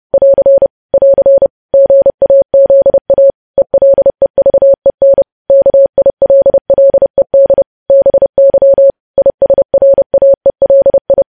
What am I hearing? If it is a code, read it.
Morse code, heard as a keyed tone, about 30 wpm, 550 Hz. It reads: <AR> <AR> GAZA ELEVEN KILLED BY ISRAELI
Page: news headlines in morse code at 30 wpm